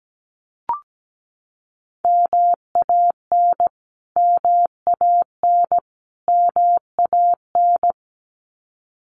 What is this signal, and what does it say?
Morse code, heard as a keyed tone, about 17 wpm, 700 Hz. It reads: MAN MAN MAN
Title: Top 300 Words Repeated 17wpm